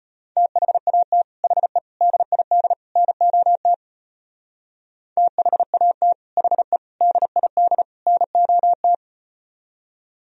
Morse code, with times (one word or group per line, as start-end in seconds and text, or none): 0.4-1.2 THAT
1.4-1.8 HE
2.0-2.7 DID
3.0-3.7 NOT
5.2-6.1 THAT
6.4-6.8 HE
7.0-7.8 DID
8.1-8.9 NOT